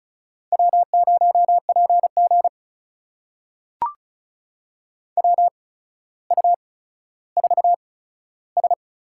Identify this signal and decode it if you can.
Morse code, heard as a keyed tone, about 36 wpm, spaced as for 15 wpm, 700 Hz. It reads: W0PG WU4S
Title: USA Call Signs - 8x Character Spacing - 35wpm